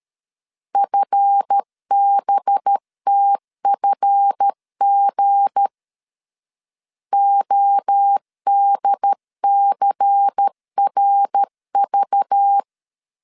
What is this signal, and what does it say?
Morse code, heard as a keyed tone, about 13 wpm, 800 Hz. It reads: FBTFG ODCRV
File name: LEZIONI / 7 GF9Z / 7 GF 12wpm 2122